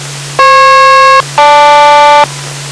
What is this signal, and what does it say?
Aircraft SELCAL, heard as two pairs of tones. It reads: FP-GL